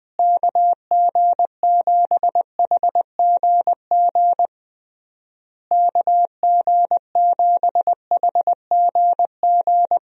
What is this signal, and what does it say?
Morse code, heard as a keyed tone, about 20 wpm, 700 Hz. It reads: KG7HGG KG7HGG
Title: USA Call Signs - 6 Characters - Repeated 3x - 20wpm